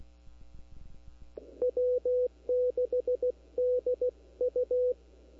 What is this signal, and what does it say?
Morse code, heard as a keyed tone, about 16 wpm, 500 Hz. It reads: W6DU
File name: CW / 2005-k5tr arrl cw dx -23:00